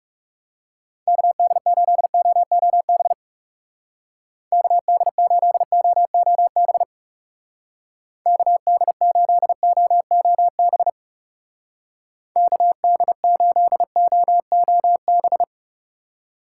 Morse code, heard as a keyed tone, about 38 wpm, 700 Hz. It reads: KD8OOB KD8OOB KD8OOB KD8OOB